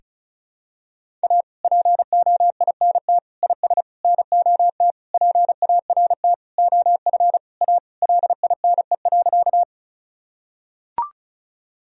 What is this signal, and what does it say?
Morse code, heard as a keyed tone, about 36 wpm, 700 Hz. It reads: A POINT IS NOT PART OF A LINE.
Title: Sentences from Top 1000 Words - Review Part 1 - 35wpm